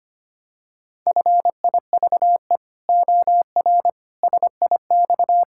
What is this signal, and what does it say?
Morse code, heard as a keyed tone, about 25 wpm, 700 Hz. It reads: FIVE OR SIX
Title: Pangram Challenge v2 - 25wpm